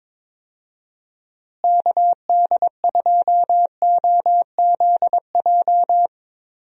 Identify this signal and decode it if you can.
Morse code, heard as a keyed tone, about 22 wpm, 700 Hz. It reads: KD2OZJ
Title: USA Call Signs - 6 Characters - Repeated 2x - 22wpm